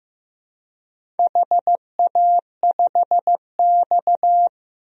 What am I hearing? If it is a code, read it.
Morse code, heard as a keyed tone, about 15 wpm, 700 Hz. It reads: HA5X
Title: DXCC-100 Call Signs - 4 Characters - Repeated 2x - 15wpm